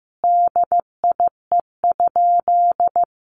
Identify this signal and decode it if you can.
Morse code, heard as a keyed tone, about 15 wpm, 700 Hz. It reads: DIE?
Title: Sentences from Top 1300 Words 15wpm